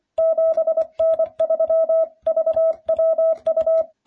Morse code, heard as a keyed tone, about 24 wpm, 660 Hz. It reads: 7N3VWU